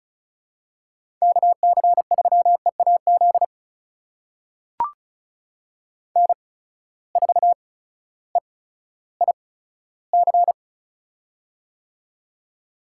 Morse code, heard as a keyed tone, about 36 wpm, spaced as for 16 wpm, 700 Hz. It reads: KC3EAZ N4EIC
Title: USA Call Signs - 8x Character Spacing - 35wpm